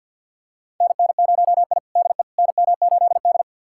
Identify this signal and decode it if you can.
Morse code, heard as a keyed tone, about 47 wpm, 700 Hz. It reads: NN0I DE NM8D
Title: USA Callsign DE Callsign - 4 Characters - 50wpm